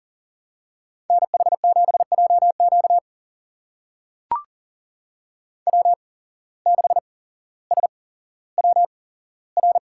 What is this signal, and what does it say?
Morse code, heard as a keyed tone, about 40 wpm, spaced as for 17 wpm, 700 Hz. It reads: NH7JQ W6SWR
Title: USA Call Signs - 5 Characters - 8x Character Spacing - 40wpm